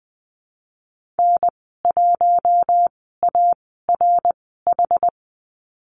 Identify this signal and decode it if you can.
Morse code, heard as a keyed tone, about 20 wpm, spaced as for 14 wpm, 700 Hz. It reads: N1ARH
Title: 122 Lesson 20g - Number 3 - Call Signs 20wpm